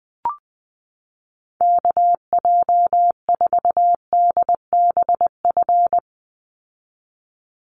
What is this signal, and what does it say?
Morse code, heard as a keyed tone, about 20 wpm, 700 Hz. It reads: KJ4DBF
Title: USA Call Signs 20wpm